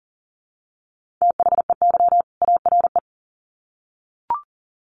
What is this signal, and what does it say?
Morse code, heard as a keyed tone, about 40 wpm, 700 Hz. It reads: THEY ARE
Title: Sets of 2 Words 40wpm